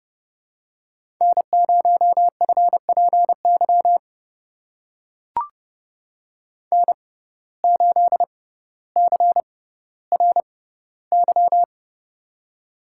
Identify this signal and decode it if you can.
Morse code, heard as a keyed tone, about 30 wpm, spaced as for 15 wpm, 700 Hz. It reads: N0FPY N8CRY
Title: USA Call Signs - 5 Characters - 6x Character Spacing - 30wpm